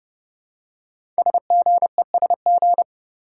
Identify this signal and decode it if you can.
Morse code, heard as a keyed tone, about 29 wpm, 700 Hz. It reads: SGESG
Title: Code Groups - 5 Letters - 30wpm